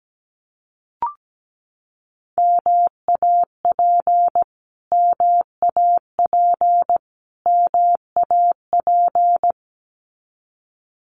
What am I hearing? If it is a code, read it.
Morse code, heard as a keyed tone, about 17 wpm, 700 Hz. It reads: MAP MAP MAP
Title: Top 500 Words Repeated 17wpm